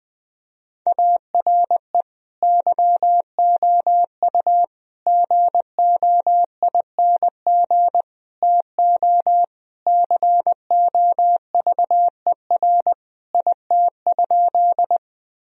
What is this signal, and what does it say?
Morse code, heard as a keyed tone, about 21 wpm, 700 Hz. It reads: ARE YOU GOING TO COVER IT?
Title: Sentences from Top 1000 Words - Review Part 1 - 20wpm